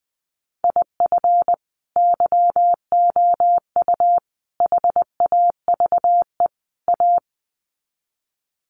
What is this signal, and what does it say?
Morse code, heard as a keyed tone, about 19 wpm, 700 Hz. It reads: IF YOU HAVE A
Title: Sets of 4 Words 20wpm